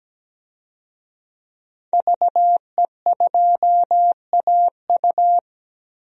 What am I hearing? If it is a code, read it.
Morse code, heard as a keyed tone, about 17 wpm, 700 Hz. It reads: VE2AU
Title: DXCC-100 Call Signs - 5 Characters - 17wpm